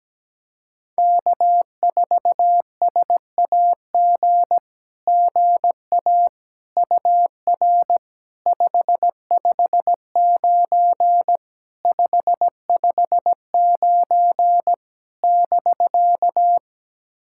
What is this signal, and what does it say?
Morse code, heard as a keyed tone, about 17 wpm, 700 Hz. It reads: K4SAG GA UR 559 559 <BK>
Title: SOTA QSO Practice 17wpm